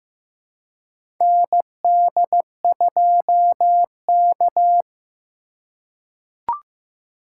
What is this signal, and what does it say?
Morse code, heard as a keyed tone, about 15 wpm, 700 Hz. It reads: ND2K
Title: USA Call Signs - 2x Character Spacing - 15wpm